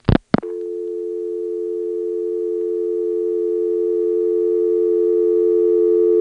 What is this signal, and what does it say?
Dial tone (North America), 350 + 440 Hz.